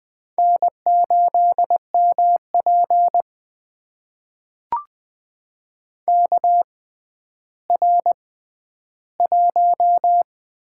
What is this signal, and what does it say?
Morse code, heard as a keyed tone, about 20 wpm, spaced as for 11 wpm, 700 Hz. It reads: N8MP KR1
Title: USA Call Signs - 4 Characters - 6x Character Spacing - 20wpm